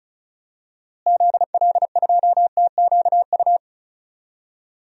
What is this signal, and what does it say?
Morse code, heard as a keyed tone, about 36 wpm, 700 Hz. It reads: ZL2TQU
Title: DXCC-100 Call Signs - 6 Characters - Repeated 2x - 35wpm